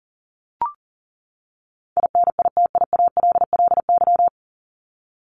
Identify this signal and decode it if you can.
Morse code, heard as a keyed tone, about 40 wpm, 700 Hz. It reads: INITIALLY
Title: Top 1251-1500 Words 40wpm